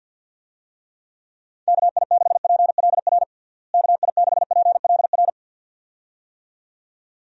Morse code, heard as a keyed tone, about 53 wpm, 700 Hz. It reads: KI6PLR KI6PLR
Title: USA Call Signs - 6 Characters - Repeated 2x - 50wpm